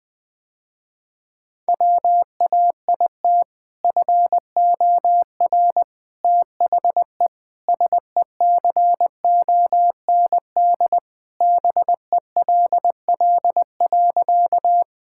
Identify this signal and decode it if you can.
Morse code, heard as a keyed tone, about 21 wpm, 700 Hz. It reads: WAIT FOR THE SECOND BELL.
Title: Sentences with 2-Word Phrases - Spoken Beginning - 20wpm